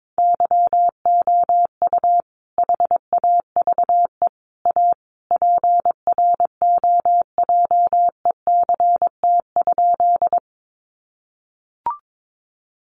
Morse code, heard as a keyed tone, about 22 wpm, 700 Hz. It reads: YOU HAVE A PROJECT?
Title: Sentences from Top 1400 Words 22wpm